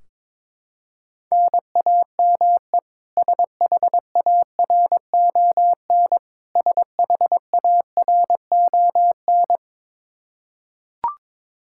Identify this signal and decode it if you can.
Morse code, heard as a keyed tone, about 22 wpm, 700 Hz. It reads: NAME SHARON SHARON
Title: K1USN SST QSO Practice 22wpm